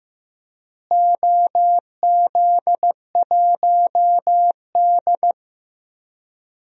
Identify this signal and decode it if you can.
Morse code, heard as a keyed tone, about 15 wpm, 700 Hz. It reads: OZ1D
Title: DXCC-100 Call Signs - 4 Characters - Repeated 2x - 15wpm